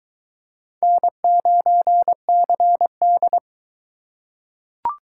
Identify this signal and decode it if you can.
Morse code, heard as a keyed tone, about 21 wpm, 700 Hz. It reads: N9CD